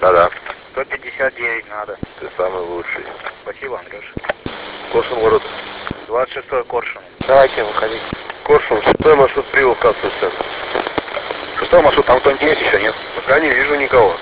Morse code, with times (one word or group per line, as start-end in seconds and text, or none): none